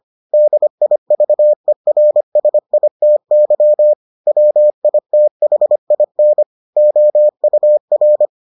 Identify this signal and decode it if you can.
Morse code, heard as a keyed tone, about 25 wpm, 600 Hz. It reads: DIVERSITY WITHIN OUR